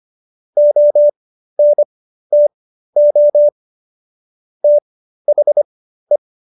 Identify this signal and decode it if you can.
Morse code, heard as a keyed tone, about 25 wpm, spaced as for 13 wpm, 590 Hz. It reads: ONTO THE